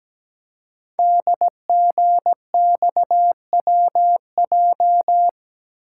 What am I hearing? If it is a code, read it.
Morse code, heard as a keyed tone, about 17 wpm, 700 Hz. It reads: DGXWJ
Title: Code Groups - 5 Letters - 17wpm